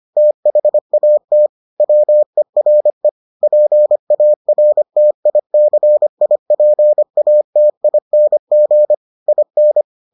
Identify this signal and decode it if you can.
Morse code, heard as a keyed tone, about 25 wpm, 600 Hz. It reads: THAT WERE PARTICIPATING IN